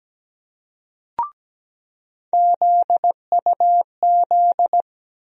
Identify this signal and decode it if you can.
Morse code, heard as a keyed tone, about 17 wpm, 700 Hz. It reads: ZUZ
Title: USA Call Sign Postfixes - 3 Letters 17wpm